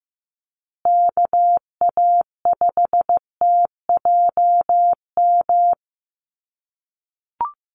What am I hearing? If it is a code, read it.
Morse code, heard as a keyed tone, about 15 wpm, 700 Hz. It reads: KA5TJM
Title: USA Call Signs Repeated 3x 15wpm